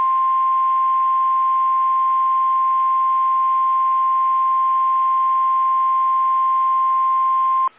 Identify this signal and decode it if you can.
Emergency Alert System (SAME) tones (the 1050 Hz attention tone). It is heard from 0.8 to 6.6 s.
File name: eas / KSDZKDJL / KSDZKDJL